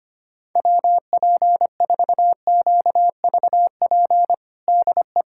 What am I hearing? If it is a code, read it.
Morse code, heard as a keyed tone, about 25 wpm, 700 Hz. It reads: WP4QVP DE
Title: USA Callsign DE Callsign 25wpm